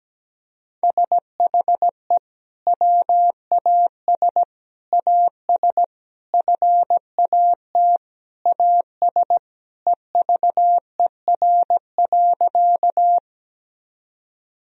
Morse code, heard as a keyed tone, about 17 wpm, 700 Hz. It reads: SHE WAS AS FAT AS EVER.
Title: Sentences from Top 1000 Words - Review Part 4 - 17wpm